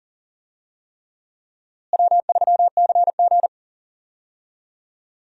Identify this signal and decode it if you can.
Morse code, heard as a keyed tone, about 40 wpm, 700 Hz. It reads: W3CG
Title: USA Call Signs - 4 Characters - Repeated 3x - 40wpm